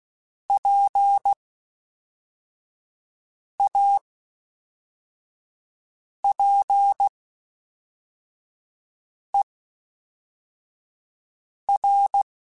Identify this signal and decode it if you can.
Morse code, heard as a keyed tone, about 16 wpm, spaced as for 4 wpm, 800 Hz. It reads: PAPER
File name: Misc / Misc-mp3s / WordPairs sp @5